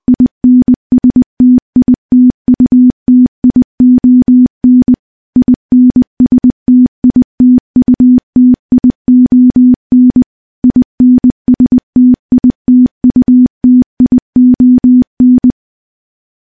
Morse code, heard as a keyed tone, about 21 wpm, 250 Hz. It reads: INSTITUTION INSTITUTION INSTITUTION
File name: morse-training-20WPM